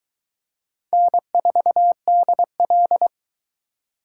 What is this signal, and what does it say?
Morse code, heard as a keyed tone, about 22 wpm, 700 Hz. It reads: N4DL